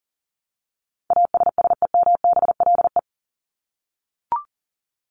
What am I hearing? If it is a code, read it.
Morse code, heard as a keyed tone, about 40 wpm, 700 Hz. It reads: ASSEMBLE
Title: LY Suffix - Top 100 Words with Base Words 40wpm